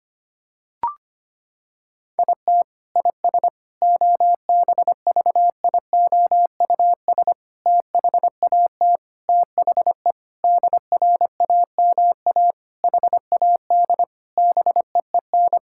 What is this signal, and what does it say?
Morse code, heard as a keyed tone, about 25 wpm, 700 Hz. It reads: IT IS OBVIOUS THAT THE DRAMA HAD BEEN
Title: Sentences from Top 1600 Words 25wpm